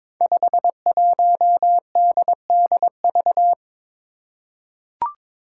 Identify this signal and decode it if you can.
Morse code, heard as a keyed tone, about 21 wpm, 700 Hz. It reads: 51DDV
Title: Five Challenging Letter-Number Combinations 22wpm